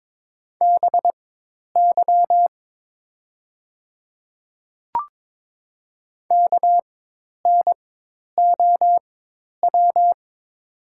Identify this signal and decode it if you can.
Morse code, heard as a keyed tone, about 22 wpm, spaced as for 10 wpm, 700 Hz. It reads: BY KNOW
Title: 4x Character Spacing - Top 200 Words - 22wpm